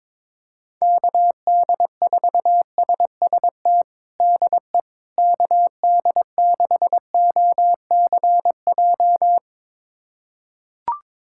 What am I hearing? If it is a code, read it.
Morse code, heard as a keyed tone, about 22 wpm, 700 Hz. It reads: KD4SST DE KD6OCJ
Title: USA Callsign DE Callsign - 6 Characters - 22wpm